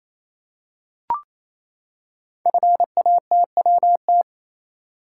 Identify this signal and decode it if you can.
Morse code, heard as a keyed tone, about 28 wpm, 690 Hz. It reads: FATWT
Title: Code Groups - 5 Letters - 28wpm